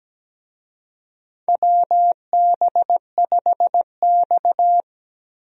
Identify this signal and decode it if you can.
Morse code, heard as a keyed tone, about 17 wpm, 700 Hz. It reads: WB5X
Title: USA Call Signs - 4 Characters - Repeated 3x - 17wpm